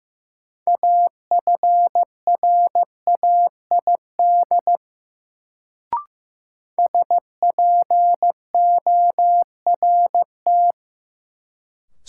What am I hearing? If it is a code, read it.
Morse code, heard as a keyed tone, about 16 wpm, 700 Hz. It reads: AFRAID SPORT
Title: Sentences from Top 1500 Words - New Word Drill Alt - 15wpm